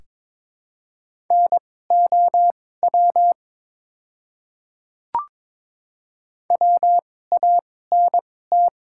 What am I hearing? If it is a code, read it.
Morse code, heard as a keyed tone, about 22 wpm, spaced as for 16 wpm, 700 Hz. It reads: NOW WANT
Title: 2x Character Spacing - Top 300 Words - 22wpm